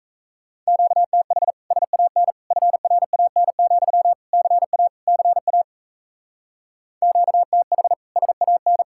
Morse code, heard as a keyed tone, about 34 wpm, 700 Hz. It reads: QTH SAN FRAN, CA CA QTH SAN
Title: Speed-Racing Common QSO Phrases - 28wpm